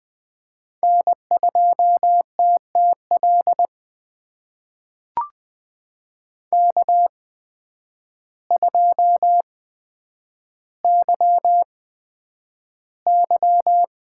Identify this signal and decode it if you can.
Morse code, heard as a keyed tone, about 20 wpm, spaced as for 9 wpm, 700 Hz. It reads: N2TTL K2YY
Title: USA Call Signs - 5 Characters - 8x Character Spacing - 20wpm